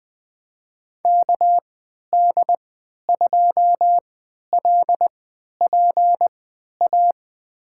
Morse code, heard as a keyed tone, about 21 wpm, spaced as for 12 wpm, 700 Hz. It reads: KD2LPA
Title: USA Call Signs - 6 Characters - 3x Character Spacing - 20wpm